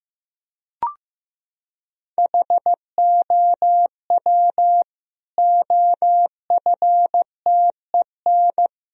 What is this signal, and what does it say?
Morse code, heard as a keyed tone, about 15 wpm, 700 Hz. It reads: HOW OFTEN
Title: Sentences from Top 1000 Words - Review Part 1 - 15wpm